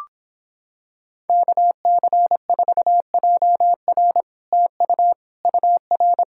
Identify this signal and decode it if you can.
Morse code, heard as a keyed tone, about 26 wpm, 700 Hz. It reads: KC4JR TU UR